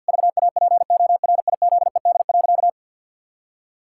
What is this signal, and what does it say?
Morse code, heard as a keyed tone, about 50 wpm, 700 Hz. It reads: VAPORIZED.
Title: Pangram Challenge v2 - 50wpm